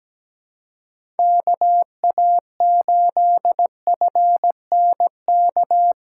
Morse code, heard as a keyed tone, about 17 wpm, 700 Hz. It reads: KA8FNK
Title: USA Callsign DE Callsign - 6 Characters - 17wpm